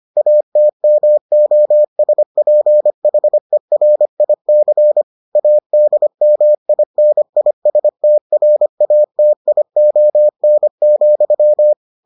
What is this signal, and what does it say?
Morse code, heard as a keyed tone, about 25 wpm, 600 Hz. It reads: ATMOSPHERIC ADMINISTRATION,